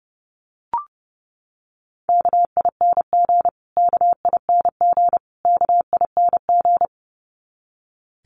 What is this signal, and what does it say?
Morse code, heard as a keyed tone, about 29 wpm, 700 Hz. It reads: KING KING KING
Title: Top 500 Words Repeated 30wpm